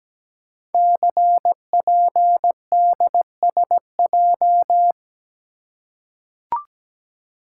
Morse code, heard as a keyed tone, about 17 wpm, 700 Hz. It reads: CPDSJ
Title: Code Groups - 5 Letter:Numbers - 17wpm